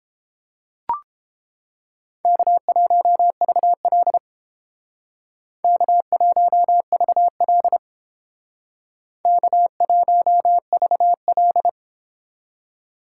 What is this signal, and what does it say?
Morse code, heard as a keyed tone, about 29 wpm, 700 Hz. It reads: K1VL K1VL K1VL